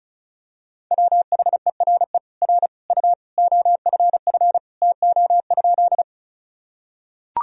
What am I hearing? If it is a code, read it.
Morse code, heard as a keyed tone, about 34 wpm, 700 Hz. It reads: WHERE R U OFF TO?
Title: Sentences from Top 200 Words and Abbreviations 35wpm